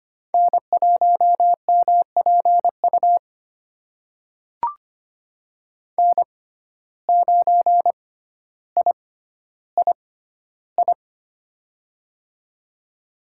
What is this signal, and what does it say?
Morse code, heard as a keyed tone, about 25 wpm, spaced as for 13 wpm, 700 Hz. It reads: N1MPU N9III
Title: USA Call Signs - 5 Characters - 6x Character Spacing - 25wpm